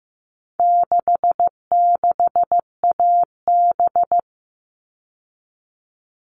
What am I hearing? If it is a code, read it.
Morse code, heard as a keyed tone, about 15 wpm, 700 Hz. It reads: 66AB
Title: All Cut Numbers - 4-Digit 15wpm